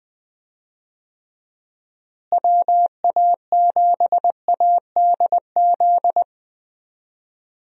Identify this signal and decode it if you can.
Morse code, heard as a keyed tone, about 21 wpm, 700 Hz. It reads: WA7ADZ
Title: USA Call Signs - 6 Characters - Repeated 2x - 20wpm